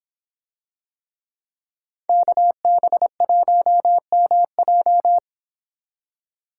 Morse code, heard as a keyed tone, about 24 wpm, 700 Hz. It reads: KB1MJ